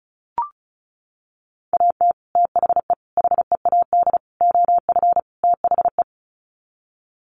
Morse code, heard as a keyed tone, about 34 wpm, 700 Hz. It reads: AT THE HEAD OF THE
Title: Sets of 5 Words 35wpm